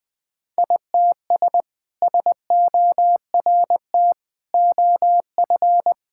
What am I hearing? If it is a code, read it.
Morse code, heard as a keyed tone, about 21 wpm, 700 Hz. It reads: ITS SORT OF
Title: Sentences with 2-Word Phrases - 20wpm